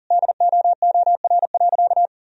Morse code, heard as a keyed tone, about 40 wpm, 700 Hz. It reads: DOOR.
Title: Sentences from Top 1000 Words - Review Part 1 - 40wpm